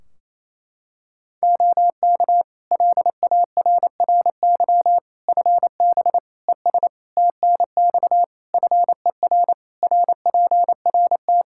Morse code, heard as a keyed tone, about 28 wpm, 700 Hz. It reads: OK LARRY FB ES TNX FER RPRT
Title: Practice QSO - Exchange 2 - Info - 28wpm